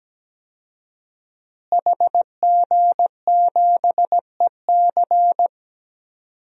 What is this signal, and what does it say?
Morse code, heard as a keyed tone, about 17 wpm, 700 Hz. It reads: HG7EC
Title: DXCC-100 Call Signs - 5 Characters - Repeated 3x - 17wpm